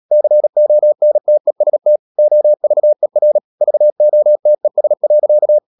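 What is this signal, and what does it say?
Morse code, heard as a keyed tone, about 37 wpm, 600 Hz. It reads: CONTEST OVER VOTES.
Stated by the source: Morse code practice at 35–39 wpm sending English text of Ken Follett's novel Never, from CW med Gnister.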